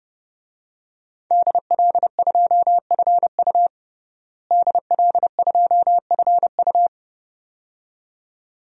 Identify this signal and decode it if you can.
Morse code, heard as a keyed tone, about 30 wpm, 700 Hz. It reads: DL2FU DL2FU
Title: DXCC-100 Call Signs - 5 Characters - Repeated 2x - 30wpm